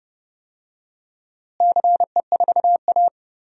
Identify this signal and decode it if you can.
Morse code, heard as a keyed tone, about 30 wpm, 700 Hz. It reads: CE4A
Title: DXCC-100 Call Signs - 4 Characters - Repeated 2x - 30wpm